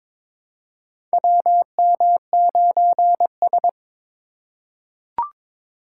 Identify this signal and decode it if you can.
Morse code, heard as a keyed tone, about 22 wpm, 700 Hz. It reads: WM9S
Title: USA Call Signs - 2x Character Spacing - 22wpm